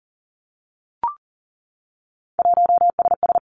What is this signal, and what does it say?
Morse code, heard as a keyed tone, about 40 wpm, 700 Hz. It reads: 1SS
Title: Three Challenging Letter-Number Combinations 40wpm